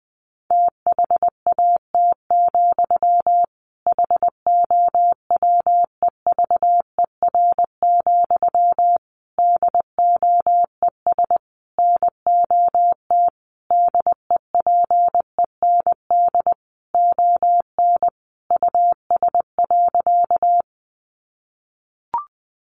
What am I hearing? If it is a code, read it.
Morse code, heard as a keyed tone, about 19 wpm, 700 Hz. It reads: THAT, HOWEVER, DOES NOT DEPEND ON US.
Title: Sentences from Top 1300 Words 20wpm